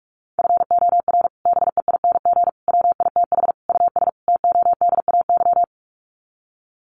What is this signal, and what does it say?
Morse code, heard as a keyed tone, about 44 wpm, 700 Hz. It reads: FOR BEING WITH US TODAY
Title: Sets of 5 Words Encore2 45wpm